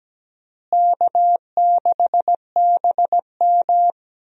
Morse code, heard as a keyed tone, about 17 wpm, 700 Hz. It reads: K6BM
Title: USA Call Signs - 4 Characters - Repeated 2x - 17wpm